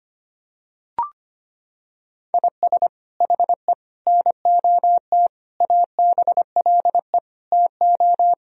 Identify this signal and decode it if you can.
Morse code, heard as a keyed tone, about 25 wpm, 700 Hz. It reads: IS HE NOT ABLE TO
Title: Sentences from Top 1000 Words - Review Part 2 - 25wpm